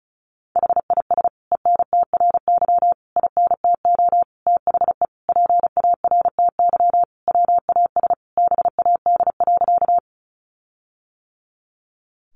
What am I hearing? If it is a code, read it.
Morse code, heard as a keyed tone, about 34 wpm, 700 Hz. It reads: HIS ENTRY INTO THE PARTY WAS BAD.
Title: Sentences from Top 1500 Words 35wpm